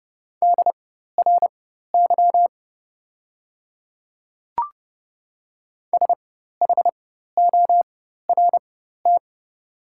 Morse code, heard as a keyed tone, about 30 wpm, spaced as for 14 wpm, 700 Hz. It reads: DRY SHORT
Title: 4x Character Spacing - Top 500 Words - 30wpm